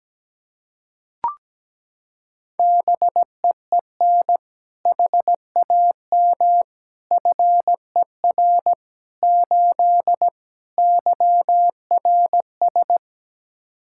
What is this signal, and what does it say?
Morse code, heard as a keyed tone, about 17 wpm, 700 Hz. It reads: BEEN HAM FER 8 YRS
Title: Practice QSO - Exchange 3 - More Info - 17wpm